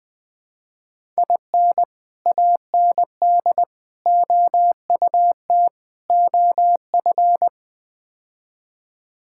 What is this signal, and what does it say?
Morse code, heard as a keyed tone, about 20 wpm, 700 Hz. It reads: IN AND OUT OF
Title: Build A Phrase - 5-Word Phrases - 20wpm